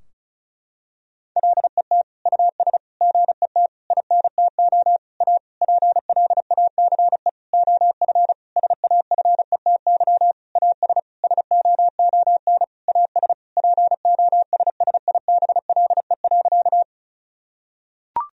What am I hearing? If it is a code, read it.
Morse code, heard as a keyed tone, about 36 wpm, 700 Hz. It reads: LET US GET INTO A PLACE OF SAFETY AS SOON AS POSSIBLE.
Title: Sentences from Top 1000 Words - Review Part 4 - 35wpm